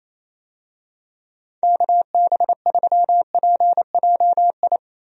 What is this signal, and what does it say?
Morse code, heard as a keyed tone, about 28 wpm, 700 Hz. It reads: KB3PJI
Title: USA Call Signs - 6 Characters - Repeated 2x - 28wpm